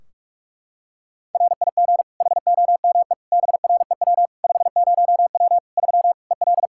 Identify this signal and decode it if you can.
Morse code, heard as a keyed tone, about 41 wpm, 700 Hz. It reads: RIG HOME BREW 50W 3 EL
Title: Practice QSO - Exchange 2 - Info - 45wpm